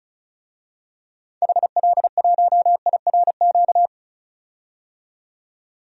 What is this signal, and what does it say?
Morse code, heard as a keyed tone, about 34 wpm, 700 Hz. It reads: HL1IRQ
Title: DXCC-100 Call Signs - 6 Characters - 35wpm